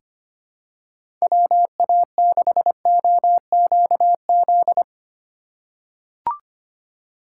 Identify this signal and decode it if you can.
Morse code, heard as a keyed tone, about 25 wpm, 700 Hz. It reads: WA6OQZ
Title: USA Call Signs - 6 Characters - 6x Character Spacing - 25wpm